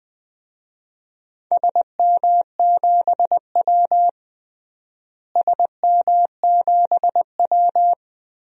Morse code, heard as a keyed tone, about 21 wpm, 700 Hz. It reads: SM7W SM7W
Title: DXCC-100 Call Signs - 4 Characters - Repeated 3x - 20wpm